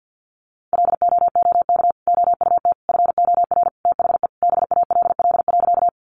Morse code, heard as a keyed tone, about 50 wpm, 700 Hz. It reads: LOOK OUT FOR THE BALL.
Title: Sentences from Top 900 Words 50wpm